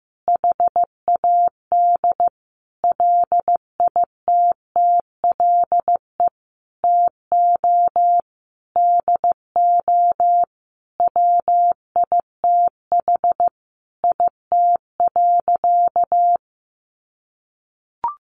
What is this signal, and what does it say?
Morse code, heard as a keyed tone, about 15 wpm, 700 Hz. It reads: HAD LITTLE TO DO WITH IT.
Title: Sentences from Top 1500 Words 15wpm